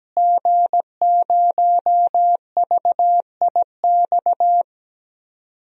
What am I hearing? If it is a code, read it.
Morse code, heard as a keyed tone, about 17 wpm, 700 Hz. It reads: G0VIX
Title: DXCC-100 Call Signs - 5 Characters - Repeated 3x - 17wpm